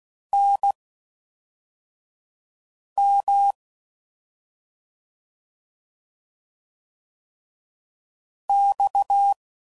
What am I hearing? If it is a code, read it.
Morse code, heard as a keyed tone, about 16 wpm, spaced as for 4 wpm, 800 Hz. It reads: NM X
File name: Mixed / CW Comparisons / CW Comparisons Broken Characters